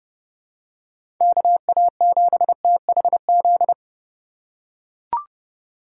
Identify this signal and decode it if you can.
Morse code, heard as a keyed tone, about 30 wpm, 700 Hz. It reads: KA7THZ